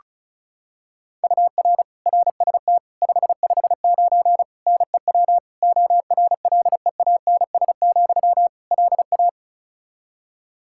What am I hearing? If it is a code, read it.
Morse code, heard as a keyed tone, about 34 wpm, 700 Hz. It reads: UR RST 559 NEW ORLEANS, LA
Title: Practice QSO - Exchange 1 - Beginning - 35wpm